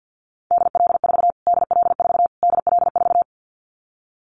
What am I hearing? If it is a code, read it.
Morse code, heard as a keyed tone, about 50 wpm, 700 Hz. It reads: DL4 DL4 DL4
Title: Callsign Prefixes for Top 25 DXCC - Repeated 3x - 50wpm